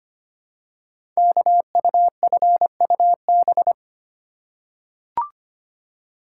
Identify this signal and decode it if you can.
Morse code, heard as a keyed tone, about 25 wpm, 700 Hz. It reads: KUFUB
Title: Code Groups - 5 Letters - 25wpm